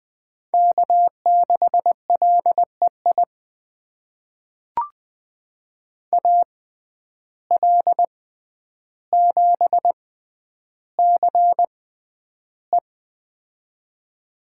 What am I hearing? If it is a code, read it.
Morse code, heard as a keyed tone, about 20 wpm, spaced as for 10 wpm, 700 Hz. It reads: K6LEI AL7CE
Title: USA Call Signs - 5 Characters - 6x Character Spacing - 20wpm